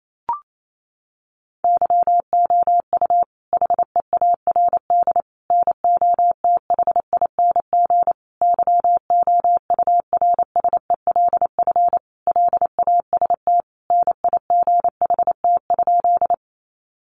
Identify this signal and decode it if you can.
Morse code, heard as a keyed tone, about 28 wpm, 700 Hz. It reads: YOU HEARD NOTHING YOURSELF LAST NIGHT?
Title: Sentences from Top 600 Words 28wpm